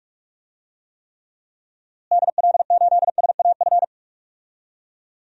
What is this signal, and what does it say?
Morse code, heard as a keyed tone, about 45 wpm, 700 Hz. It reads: DL8SAF
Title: DXCC-100 Call Signs - 6 Characters - Repeated 3x - 45wpm